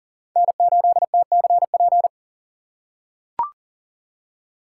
Morse code, heard as a keyed tone, about 40 wpm, 700 Hz. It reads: N8TCP
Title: USA Call Signs - 5 Characters - 4x Character Spacing - 40wpm